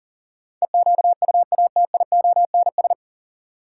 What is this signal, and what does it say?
Morse code, heard as a keyed tone, about 40 wpm, 700 Hz. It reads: EQUATIONS
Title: Sets of 5 Words - New Word Drill - 40wpm